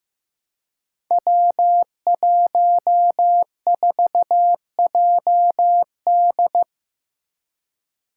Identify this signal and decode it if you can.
Morse code, heard as a keyed tone, about 15 wpm, 700 Hz. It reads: W14JD
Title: Five Challenging Letter-Number Combinations 15wpm